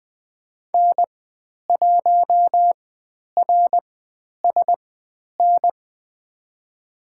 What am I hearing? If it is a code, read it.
Morse code, heard as a keyed tone, about 20 wpm, spaced as for 10 wpm, 700 Hz. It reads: N1RSN
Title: 109 Lesson 18g - Letter m - Call Signs 20wpm